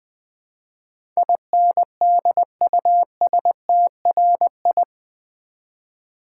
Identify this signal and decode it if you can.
Morse code, heard as a keyed tone, about 20 wpm, 700 Hz. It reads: INDUSTRI
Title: Build A Word - 4 Letters At A Time - 10-Letter Words - 20wpm